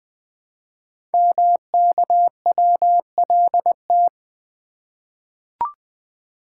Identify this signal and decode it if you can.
Morse code, heard as a keyed tone, about 20 wpm, 700 Hz. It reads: MKWLT
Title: Code Groups - 5 Letter:Numbers - 20wpm